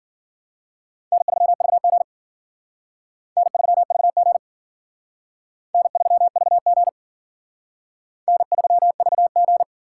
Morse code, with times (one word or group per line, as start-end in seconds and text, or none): 1.1-2.0 N3VG
3.4-4.4 N3VG
5.7-6.9 N3VG
8.3-9.6 N3VG